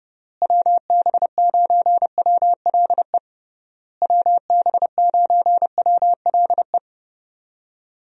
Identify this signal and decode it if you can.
Morse code, heard as a keyed tone, about 31 wpm, 700 Hz. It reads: WB9WLE WB9WLE
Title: USA Call Signs - 6 Characters - Repeated 3x - 30wpm